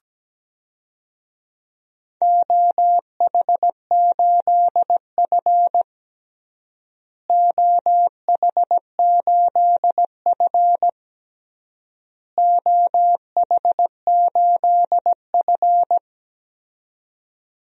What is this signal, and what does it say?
Morse code, heard as a keyed tone, about 17 wpm, 700 Hz. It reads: OH8F OH8F OH8F
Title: DXCC-100 Call Signs - 4 Characters - Repeated 3x - 17wpm